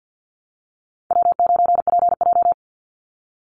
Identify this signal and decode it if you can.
Morse code, heard as a keyed tone, about 50 wpm, 700 Hz. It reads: W9PJ
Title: Call Signs Repeated 3x - 50wpm